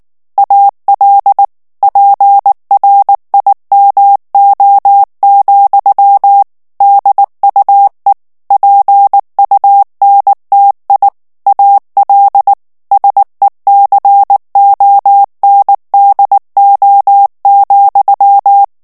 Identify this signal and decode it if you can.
Morse code, heard as a keyed tone, about 19 wpm, 800 Hz. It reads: AL PRIMO, DUE PUNTI AL SECONDO,
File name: morse mp3 / ITALIEN texte clair 2 vitesse 18 mots minute